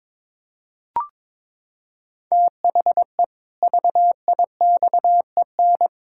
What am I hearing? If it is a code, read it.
Morse code, heard as a keyed tone, about 22 wpm, 700 Hz. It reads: THE VIXEN
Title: Pangram Challenge v2 - 22wpm